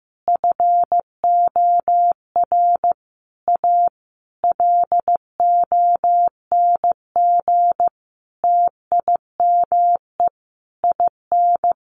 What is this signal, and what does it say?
Morse code, heard as a keyed tone, about 15 wpm, 700 Hz. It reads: FOR A LONG TIME IN
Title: Sentences from Top 900 Words 15wpm